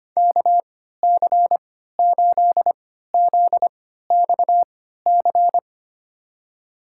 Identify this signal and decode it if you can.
Morse code, heard as a keyed tone, about 25 wpm, spaced as for 14 wpm, 700 Hz. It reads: KC8ZXC
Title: USA Call Signs - 6 Characters - 3x Character Spacing - 25wpm